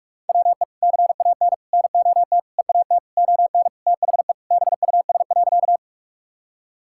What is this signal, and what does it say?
Morse code, heard as a keyed tone, about 46 wpm, 700 Hz. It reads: WE CAN NOT EAT ON THE BUS.
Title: Sentences from Top 1000 Words - Review Part 6 - 45wpm